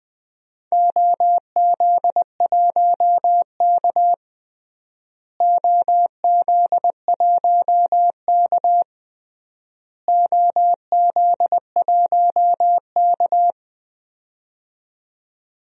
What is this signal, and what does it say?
Morse code, heard as a keyed tone, about 20 wpm, 700 Hz. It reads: OZ1K OZ1K OZ1K
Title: DXCC-100 Call Signs - 4 Characters - Repeated 3x - 20wpm